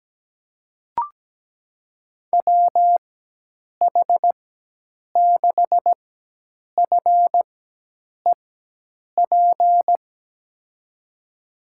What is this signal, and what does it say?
Morse code, heard as a keyed tone, about 17 wpm, spaced as for 8 wpm, 700 Hz. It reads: WH6FEP
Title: USA Call Signs - 4x Character Spacing - 17wpm